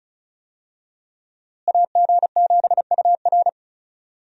Morse code, heard as a keyed tone, about 36 wpm, 700 Hz. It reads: AG7UR
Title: USA Call Signs - 5 Characters - Repeated 2x - 35wpm